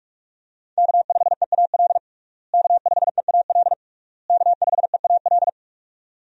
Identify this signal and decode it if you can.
Morse code, heard as a keyed tone, about 46 wpm, 700 Hz. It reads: K5EAL K5EAL K5EAL
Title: USA Call Signs - 5 Characters - Repeated 3x - 45wpm